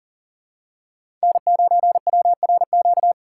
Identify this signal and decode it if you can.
Morse code, heard as a keyed tone, about 40 wpm, 700 Hz. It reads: N9WRQ
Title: USA Call Signs - 5 Characters - 2x Character Spacing - 40wpm